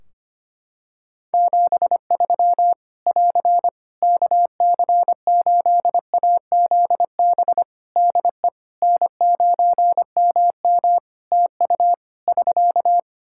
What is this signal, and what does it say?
Morse code, heard as a keyed tone, about 25 wpm, 700 Hz. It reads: 73 <AR> KC8AZB DE N9MM TU <SK>
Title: K1USN SST QSO Practice 25wpm